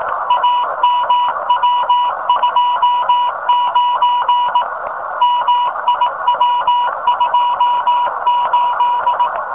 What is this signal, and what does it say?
Morse code, heard as a keyed tone, about 18 wpm, 1000 Hz, with noise. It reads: AMW29 MIW28
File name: rs15 telem